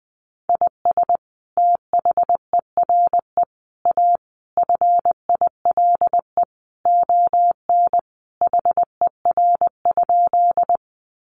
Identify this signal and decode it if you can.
Morse code, heard as a keyed tone, about 20 wpm, 700 Hz. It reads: IS THERE A FILE ON HER?
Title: Sentences from Top 1200 Words 20wpm